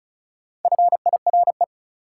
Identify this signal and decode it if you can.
Morse code, heard as a keyed tone, about 36 wpm, 700 Hz. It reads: FIRE
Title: Sentences from Top 400 Words - New Word Drill - 35wpm